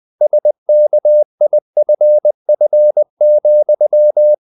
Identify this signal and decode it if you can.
Morse code, heard as a keyed tone, about 20 wpm, 600 Hz. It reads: SKIFF,
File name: the old man and the sea / the old man and the sea part 1 20wpm